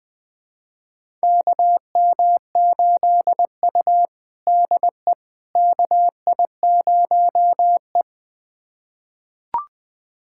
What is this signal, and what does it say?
Morse code, heard as a keyed tone, about 20 wpm, 700 Hz. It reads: KM8U DE KI0E
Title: USA Callsign DE Callsign - 4 Characters - 20wpm